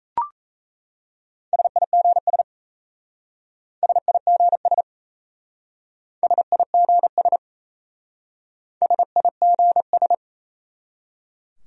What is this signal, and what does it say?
Morse code, heard as a keyed tone, about 35 wpm, 700 Hz. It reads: SIGS SIGS SIGS SIGS